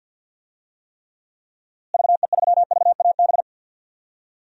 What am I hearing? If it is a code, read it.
Morse code, heard as a keyed tone, about 47 wpm, 700 Hz. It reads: VE3VAB
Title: DXCC-100 Call Signs - 6 Characters - Repeated 3x - 50wpm